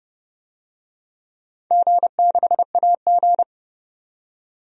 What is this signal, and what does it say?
Morse code, heard as a keyed tone, about 30 wpm, 700 Hz. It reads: G6AG
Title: DXCC-100 Call Signs - 4 Characters - 30wpm